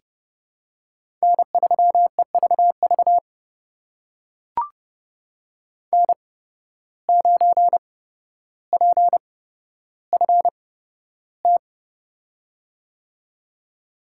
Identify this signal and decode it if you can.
Morse code, heard as a keyed tone, about 30 wpm, spaced as for 13 wpm, 700 Hz. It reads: N3EVV N9PFT
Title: USA Call Signs - 5 Characters - 8x Character Spacing - 30wpm